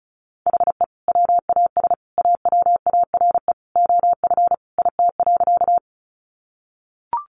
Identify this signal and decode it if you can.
Morse code, heard as a keyed tone, about 34 wpm, 700 Hz. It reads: HE WAS AWARE OF IT.
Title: Sentences from Top 1200 Words 35wpm